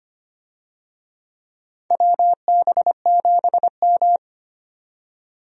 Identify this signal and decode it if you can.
Morse code, heard as a keyed tone, about 25 wpm, 700 Hz. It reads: WB7M
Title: USA Call Signs - 4 Characters - 25wpm